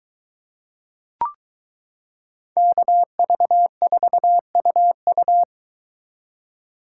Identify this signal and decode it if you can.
Morse code, heard as a keyed tone, about 22 wpm, 700 Hz. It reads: KV4UU